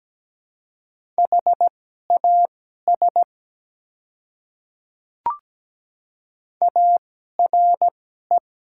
Morse code, heard as a keyed tone, about 17 wpm, spaced as for 12 wpm, 690 Hz. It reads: HAS ARE
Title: 2x Character Spacing - Top 100 Words - 17wpm